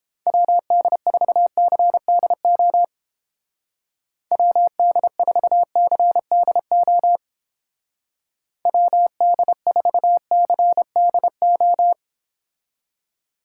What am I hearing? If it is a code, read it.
Morse code, heard as a keyed tone, about 28 wpm, 700 Hz. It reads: WD4CDO WD4CDO WD4CDO